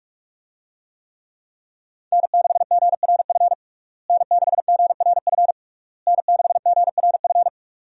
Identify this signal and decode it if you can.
Morse code, heard as a keyed tone, about 41 wpm, 700 Hz. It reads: N6GRF N6GRF N6GRF
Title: USA Call Signs - 5 Characters - Repeated 3x - 45wpm